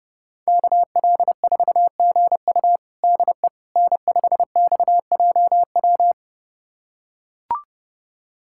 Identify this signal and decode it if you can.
Morse code, heard as a keyed tone, about 30 wpm, 700 Hz. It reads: KL4GU DE N5XJW
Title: USA Callsign DE Callsign - 5 Characters - 30wpm